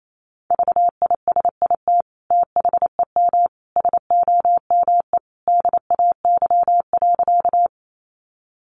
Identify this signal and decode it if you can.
Morse code, heard as a keyed tone, about 28 wpm, 700 Hz. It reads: VISIT THEM SOME DAY.
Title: Sentences from Top 900 Words 28wpm